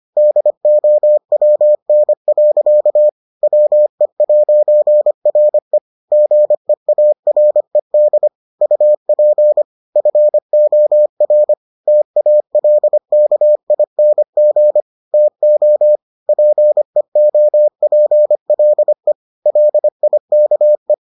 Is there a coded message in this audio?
Yes: Morse code from 0.2 to 20.9 s, heard as a keyed tone, about 25 wpm, 600 Hz. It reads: DOWN. WE'RE GEARED UP FOR TALKING TO PEOPLE LIKE